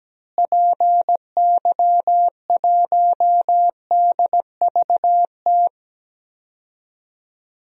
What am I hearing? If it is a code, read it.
Morse code, heard as a keyed tone, about 17 wpm, 700 Hz. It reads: PY1DVT
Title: DXCC-100 Call Signs - 6 Characters - Repeated 3x - 17wpm